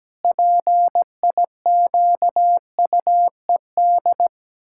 Morse code, heard as a keyed tone, about 17 wpm, 700 Hz. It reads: PIQUED
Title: Pangram Challenge v2 - 17wpm